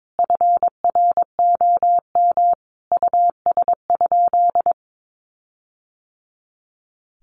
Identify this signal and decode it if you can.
Morse code, heard as a keyed tone, about 22 wpm, 700 Hz. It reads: FROM US?
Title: Sentences from Top 1500 Words 22wpm